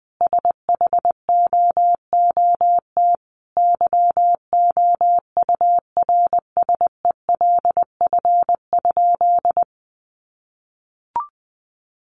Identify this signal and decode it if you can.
Morse code, heard as a keyed tone, about 20 wpm, 700 Hz. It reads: SHOOT YOURSELF?
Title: Sentences from Top 1200 Words 20wpm